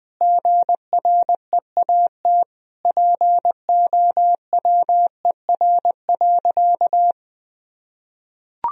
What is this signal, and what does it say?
Morse code, heard as a keyed tone, about 20 wpm, 700 Hz. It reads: GREAT POWER.
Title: Sentences from Top 1000 Words - Review Part 4 - 20wpm